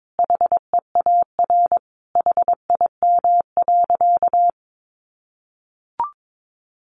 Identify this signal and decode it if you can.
Morse code, heard as a keyed tone, about 22 wpm, 700 Hz. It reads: HEAR HIM.
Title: Sentences from Top 1000 Words 22wpm